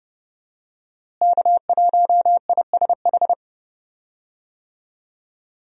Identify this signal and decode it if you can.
Morse code, heard as a keyed tone, about 30 wpm, 700 Hz. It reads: K1ISH
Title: USA Call Signs - 5 Characters - Repeated 3x - 30wpm